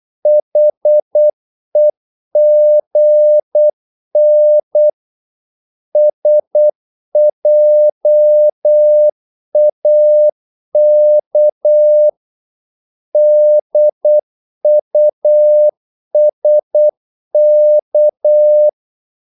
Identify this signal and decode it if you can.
Morse code, heard as a keyed tone, about 8 wpm, 600 Hz. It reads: HEGN SJAK DUSK
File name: Korte ord DK 8wpm